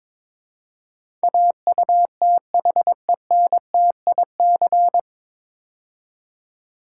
Morse code, heard as a keyed tone, about 22 wpm, 700 Hz. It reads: AUTHENTIC
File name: 9 Letter Words - First 3 Letters Given - 22wpm